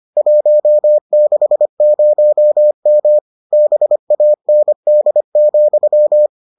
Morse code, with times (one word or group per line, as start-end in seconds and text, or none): 0.2-3.2 160M
3.5-6.3 BAND,